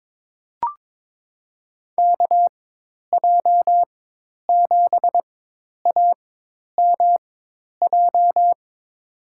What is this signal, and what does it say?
Morse code, heard as a keyed tone, about 23 wpm, spaced as for 10 wpm, 700 Hz. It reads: KJ7AMJ
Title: USA Call Signs - 6 Characters - 4x Character Spacing - 22wpm